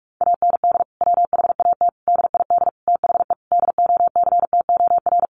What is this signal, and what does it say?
Morse code, heard as a keyed tone, about 44 wpm, 700 Hz. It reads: AND WHAT DID THE DOCTOR
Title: Sentences from Top 1500 Words 45wpm